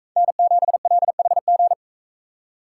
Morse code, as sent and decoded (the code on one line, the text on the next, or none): -. --... .-.. .... --.
N7LHG